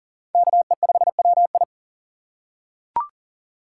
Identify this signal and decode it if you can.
Morse code, heard as a keyed tone, about 40 wpm, 700 Hz. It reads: KE5WI
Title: USA Call Signs - 5 Characters - 4x Character Spacing - 40wpm